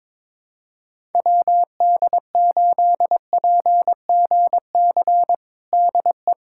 Morse code, heard as a keyed tone, about 22 wpm, 700 Hz. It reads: WD8PGC DE
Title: USA Callsign DE Callsign - 6 Characters - 22wpm